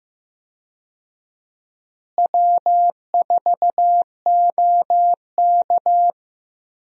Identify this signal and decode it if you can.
Morse code, heard as a keyed tone, about 15 wpm, 700 Hz. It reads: W4OK
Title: USA Call Signs - 4 Characters - Repeated 3x - 15wpm